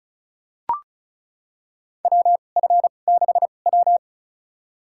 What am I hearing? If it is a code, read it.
Morse code, heard as a keyed tone, about 34 wpm, spaced as for 25 wpm, 700 Hz. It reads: WF6W
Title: USA Call Signs - 4 Characters - 2x Character Spacing - 35wpm